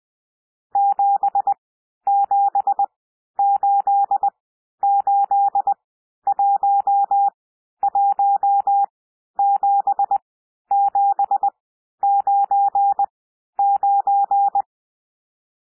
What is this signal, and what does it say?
Morse code, heard as a keyed tone, about 20 wpm, spaced as for 11 wpm, 800 Hz. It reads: 7788117799
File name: cwmp3 / N1 SOUND NUMBERS 20 wpm